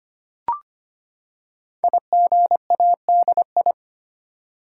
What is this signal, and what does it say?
Morse code, heard as a keyed tone, about 25 wpm, 700 Hz. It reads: IGADI
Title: Code Groups - 5 Letters - 25wpm